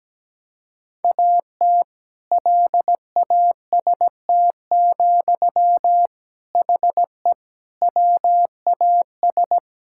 Morse code, heard as a keyed tone, about 17 wpm, 700 Hz. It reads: AT LAST, HE WAS
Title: Sentences from Top 1000 Words - Review Part 6 - 17wpm